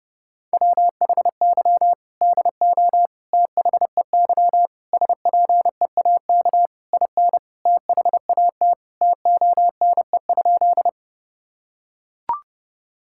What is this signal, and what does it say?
Morse code, heard as a keyed tone, about 32 wpm, 700 Hz. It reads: WHY DO THEY SPEAK IN THAT TONE?
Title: Sentences from Top 1600 Words 30wpm